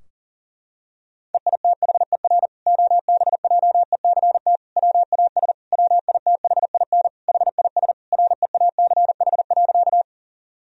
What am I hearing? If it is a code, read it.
Morse code, heard as a keyed tone, about 40 wpm, 700 Hz. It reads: EITHER OBJECT WAS WITHIN HIS REACH.
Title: Sentences from Top 1000 Words - Review Part 4 - 40wpm